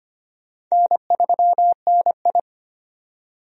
Morse code, heard as a keyed tone, about 25 wpm, 700 Hz. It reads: N3NI
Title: USA Call Signs - 4 Characters - 25wpm